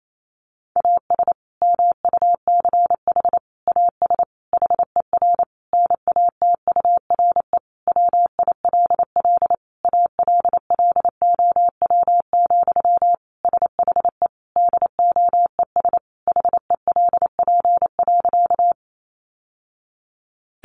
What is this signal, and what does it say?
Morse code, heard as a keyed tone, about 28 wpm, 700 Hz. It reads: AS MUCH AS HER NATURE WILL ALLOW, SHE DOES HELP.
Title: Sentences from Top 700 Words 28wpm